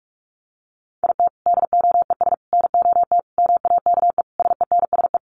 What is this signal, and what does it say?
Morse code, heard as a keyed tone, about 44 wpm, 700 Hz. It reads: IT DOES NOT MAKE SENSE
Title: Sets of 5 Words Encore2 45wpm